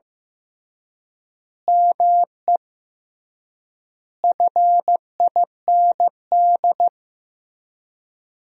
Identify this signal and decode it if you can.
Morse code, heard as a keyed tone, about 15 wpm, 700 Hz. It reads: ME FIND
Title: 3x Word-Spacing - Sentences from Top 200 Words - 15wpm